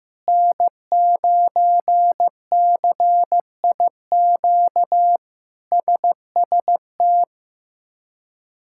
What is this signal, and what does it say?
Morse code, heard as a keyed tone, about 15 wpm, 700 Hz. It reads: N9CIQ SST
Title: K1USN SST QSO Practice - 15wpm